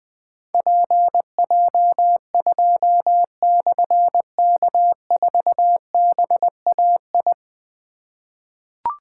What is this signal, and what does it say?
Morse code, heard as a keyed tone, about 20 wpm, 700 Hz. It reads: PJ2/K4BAI
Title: Difficult Call Signs 20wpm